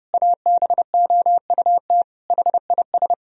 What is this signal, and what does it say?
Morse code, heard as a keyed tone, about 30 wpm, 700 Hz. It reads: ABOUT HIS
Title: Sentences from Top 1000 Words - Review Part 4 - 30wpm